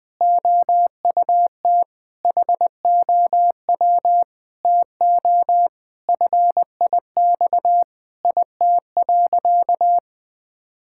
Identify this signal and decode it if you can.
Morse code, heard as a keyed tone, about 20 wpm, 700 Hz. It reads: OUT HOW TO FIX IT.
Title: Sentences with 6-Word Phrases - 20wpm